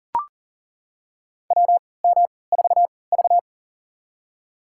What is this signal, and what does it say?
Morse code, heard as a keyed tone, about 40 wpm, spaced as for 23 wpm, 700 Hz. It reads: WM4V
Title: USA Call Signs - 4 Characters - 3x Character Spacing - 40wpm